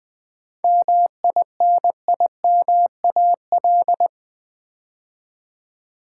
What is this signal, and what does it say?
Morse code, heard as a keyed tone, about 20 wpm, 700 Hz. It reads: MINIMAL
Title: 7 Letter Words - First 3 Letters Given - 20wpm